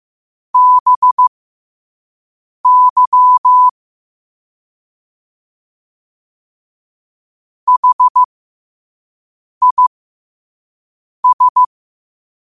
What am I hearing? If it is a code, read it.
Morse code, heard as a keyed tone, about 15 wpm, spaced as for 5 wpm, 1000 Hz. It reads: BY HIS
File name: qotdfiles / 5wpm